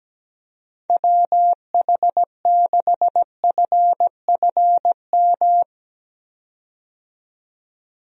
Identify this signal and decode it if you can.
Morse code, heard as a keyed tone, about 17 wpm, 700 Hz. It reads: WH6FFM
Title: USA Call Signs - 6 Characters - 17wpm